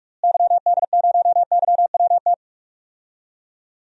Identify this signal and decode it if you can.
Morse code, heard as a keyed tone, about 44 wpm, 700 Hz. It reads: YD0YWT
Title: DXCC-100 Call Signs - 6 Characters - Repeated 3x - 45wpm